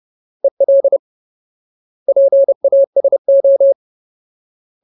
Morse code, heard as a keyed tone, about 30 wpm, 550 Hz. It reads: EL PASO
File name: Ultimate Word List / 30WPM Ultimate